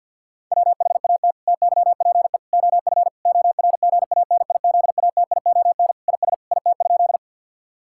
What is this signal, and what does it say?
Morse code, heard as a keyed tone, about 50 wpm, 700 Hz. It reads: WHAT TYPE OF ORGANIZATION IS IT?
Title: Sentences from Top 1000 Words - Review Part 5 - 50wpm